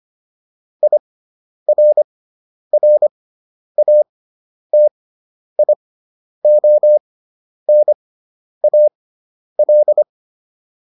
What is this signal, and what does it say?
Morse code, heard as a keyed tone, about 25 wpm, spaced as for 10 wpm, 600 Hz. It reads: IRRATIONAL